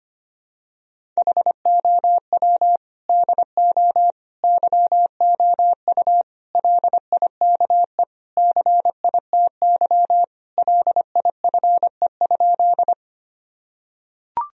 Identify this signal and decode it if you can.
Morse code, heard as a keyed tone, about 25 wpm, 700 Hz. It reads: HOW DO YOU LIKE CITY LIFE?
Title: Sentences from Top 1000 Words - Review Part 1 - 25wpm